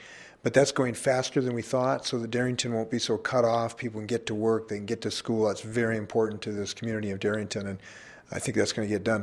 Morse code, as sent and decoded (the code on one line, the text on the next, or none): none